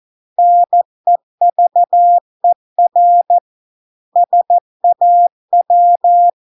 Morse code, heard as a keyed tone, about 14 wpm, 700 Hz. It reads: NEVER SAW